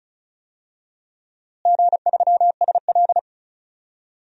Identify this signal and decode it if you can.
Morse code, heard as a keyed tone, about 34 wpm, 700 Hz. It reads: G3SL
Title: DXCC-100 Call Signs - 4 Characters - Repeated 2x - 35wpm